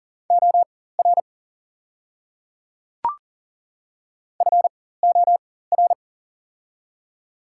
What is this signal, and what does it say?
Morse code, heard as a keyed tone, about 40 wpm, spaced as for 19 wpm, 700 Hz. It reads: OR FOR
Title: 4x Character Spacing - Top 500 Words - 40wpm